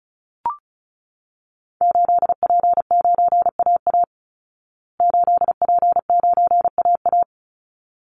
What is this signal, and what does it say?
Morse code, heard as a keyed tone, about 34 wpm, 700 Hz. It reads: 8P9AA 8P9AA
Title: Call Signs Repeated 3x - 35wpm